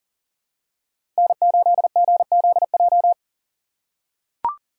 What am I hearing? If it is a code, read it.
Morse code, heard as a keyed tone, about 40 wpm, 700 Hz. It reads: N8GZJ
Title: USA Call Signs - 5 Characters - 4x Character Spacing - 40wpm